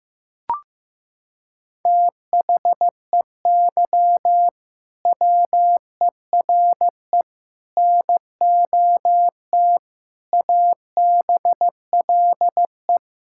Morse code, heard as a keyed tone, about 15 wpm, 700 Hz. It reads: THEY WERE NOT ABLE
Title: Sentences from Top 1000 Words - Review Part 6 - 15wpm